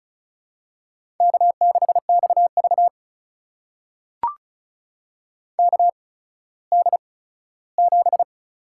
Morse code, heard as a keyed tone, about 36 wpm, spaced as for 17 wpm, 700 Hz. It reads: K6XV KD7
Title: USA Call Signs - 8x Character Spacing - 35wpm